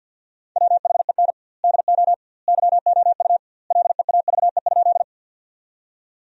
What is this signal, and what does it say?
Morse code, heard as a keyed tone, about 50 wpm, 700 Hz. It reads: WHEN DO YOU LEAVE?
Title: 2x Word-Spacing - Sentences from Top 400 Words - 50wpm